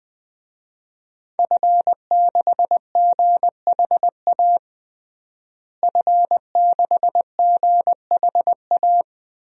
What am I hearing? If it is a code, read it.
Morse code, heard as a keyed tone, about 20 wpm, 700 Hz. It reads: F6GHA F6GHA
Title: DXCC-100 Call Signs - 5 Characters - Repeated 2x - 20wpm